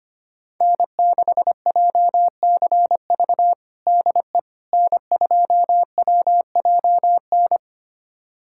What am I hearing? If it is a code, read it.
Morse code, heard as a keyed tone, about 25 wpm, 700 Hz. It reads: N6JCV DE N2WJN
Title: USA Callsign DE Callsign - 5 Characters - 25wpm